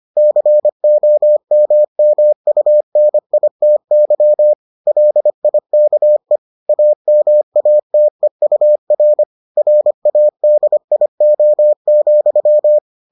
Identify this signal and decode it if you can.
Morse code, heard as a keyed tone, about 25 wpm, 600 Hz. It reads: COMMUNITY LIKE AMATEUR RADIO,